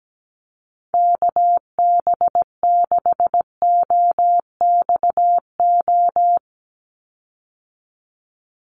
Morse code, heard as a keyed tone, about 17 wpm, 700 Hz. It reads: KB6OXO
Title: USA Call Signs 17wpm